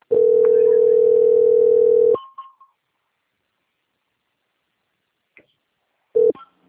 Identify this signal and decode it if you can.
Ringback tone (North America), 440 + 480 Hz.
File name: out- - - - - .7343